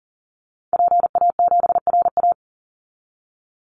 Morse code, heard as a keyed tone, about 39 wpm, 700 Hz. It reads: PA7RA
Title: Call Signs Repeated 3x - 40wpm